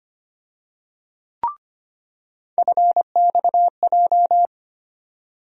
Morse code, heard as a keyed tone, about 25 wpm, 700 Hz. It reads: FXJ
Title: USA Call Sign Postfixes - 3 Letters 25wpm